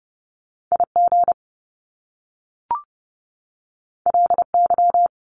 Morse code, heard as a keyed tone, about 30 wpm, 700 Hz. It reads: IG LY
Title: Top Two Letter Combinations in Words 30wpm